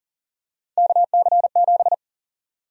Morse code, heard as a keyed tone, about 40 wpm, 700 Hz. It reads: KC7
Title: USA Call Sign Prefixes - Repeated 2x - 40wpm